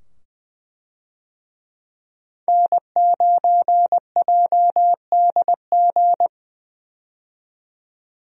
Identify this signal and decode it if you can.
Morse code, heard as a keyed tone, about 20 wpm, 700 Hz. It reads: N9JDG
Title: USA Call Signs - 5 Characters - 20wpm